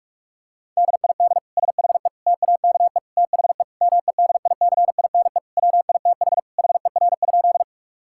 Morse code, heard as a keyed tone, about 41 wpm, 700 Hz. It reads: DID SHE TAKE THE MEDICINE WITH HER?
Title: Sentences from Top 1000 Words - Review Part 6 - 45wpm